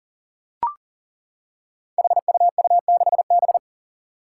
Morse code, heard as a keyed tone, about 39 wpm, 700 Hz. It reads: HUU6B
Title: Code Groups - 5 Letter:Numbers - 40wpm